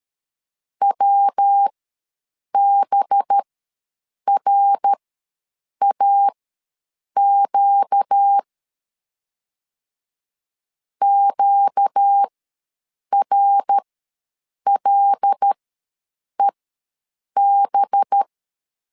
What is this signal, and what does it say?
Morse code, heard as a keyed tone, about 13 wpm, spaced as for 7 wpm, 800 Hz. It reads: WBRAQ QRLEB
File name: LEZIONI / 5 LBVC / 5 LB ws 12wpm 3034